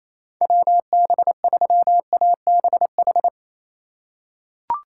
Morse code, heard as a keyed tone, about 28 wpm, 700 Hz. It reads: WB3ABH
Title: USA Call Signs - 4x Character Spacing - 28wpm